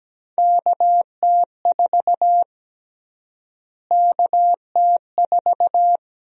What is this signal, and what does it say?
Morse code, heard as a keyed tone, about 17 wpm, 700 Hz. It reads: KT4 KT4
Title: USA Call Sign Prefixes - Repeated 2x - 17wpm